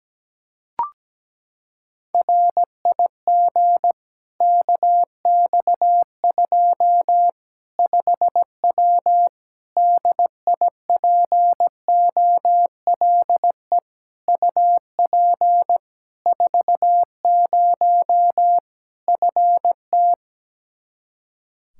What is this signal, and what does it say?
Morse code, heard as a keyed tone, about 17 wpm, 700 Hz. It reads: RIG KX2 5W DIPOLE UP 40 FT
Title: Practice QSO - Exchange 2 - Info - 17wpm